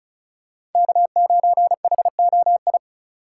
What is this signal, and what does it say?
Morse code, heard as a keyed tone, about 34 wpm, 700 Hz. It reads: K9HOI
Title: USA Call Signs - 5 Characters - 2x Character Spacing - 35wpm